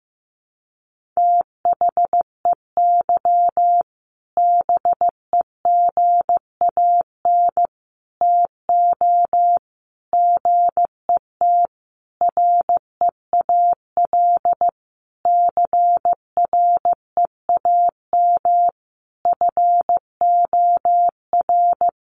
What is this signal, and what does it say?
Morse code, heard as a keyed tone, about 15 wpm, 700 Hz. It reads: THEY BEGAN TO GET REAL CREAM FOR
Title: Sentences from Top 1500 Words 15wpm